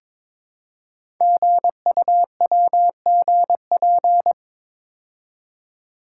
Morse code, heard as a keyed tone, about 23 wpm, 700 Hz. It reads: GUWGP
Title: Code Groups - 5 Letters - 22wpm